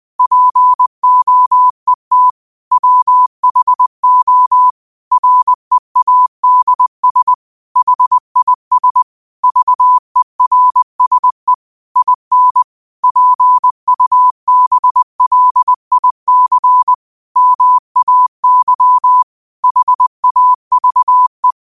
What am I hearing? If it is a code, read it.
Morse code, heard as a keyed tone, about 20 wpm, 1000 Hz. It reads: POET WHO READS HIS VERSE IN PUBLIC MAY HAVE
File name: qotdfiles / 20wpm